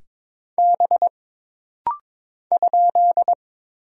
Morse code, heard as a keyed tone, about 22 wpm, 700 Hz. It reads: B ?
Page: This rapid-fire timing is geared to push your ICR (Instant Character Recognition) skills.